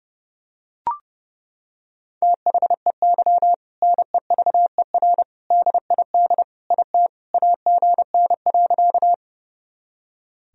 Morse code, heard as a keyed tone, about 30 wpm, 700 Hz. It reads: THEY NEVER DID IT AGN.
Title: Sentences from Top 300 Words and Abbreviations 30wpm